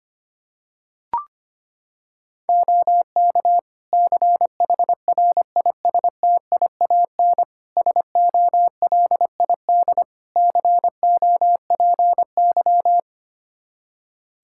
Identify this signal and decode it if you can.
Morse code, heard as a keyed tone, about 25 wpm, 700 Hz. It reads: OK CHRISTIAN SOLID COPY
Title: Practice QSO - Exchange 3 - More Info - 25wpm